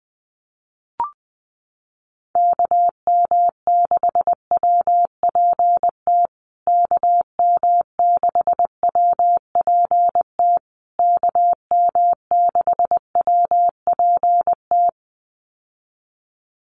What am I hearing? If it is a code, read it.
Morse code, heard as a keyed tone, about 20 wpm, 700 Hz. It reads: KM6WPT KM6WPT KM6WPT
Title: USA Call Signs Repeated 3x 20wpm